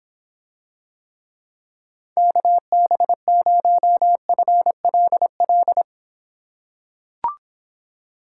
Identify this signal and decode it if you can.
Morse code, heard as a keyed tone, about 24 wpm, 700 Hz. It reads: KB0FLL